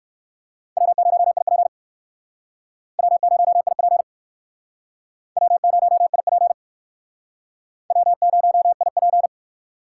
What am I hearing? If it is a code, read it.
Morse code, heard as a keyed tone, about 61 wpm, 700 Hz. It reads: W0IP W0IP W0IP W0IP